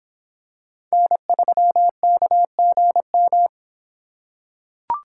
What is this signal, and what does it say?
Morse code, heard as a keyed tone, about 25 wpm, 700 Hz. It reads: N3KGM